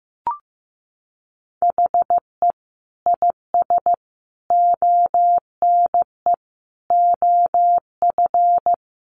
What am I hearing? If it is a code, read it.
Morse code, heard as a keyed tone, about 15 wpm, 700 Hz. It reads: HE IS ONE OF
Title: Sets of 5 Words Encore 15wpm